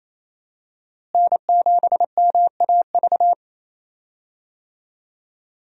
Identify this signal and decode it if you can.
Morse code, heard as a keyed tone, about 27 wpm, 700 Hz. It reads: N7MAV
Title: USA Call Signs - 5 Characters - Repeated 3x - 28wpm